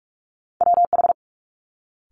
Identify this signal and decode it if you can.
Morse code, heard as a keyed tone, about 46 wpm, 700 Hz. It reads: WH
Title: Multi-letter Phonemes Quick Fire 45wpm